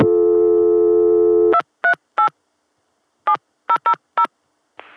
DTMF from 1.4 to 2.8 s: 238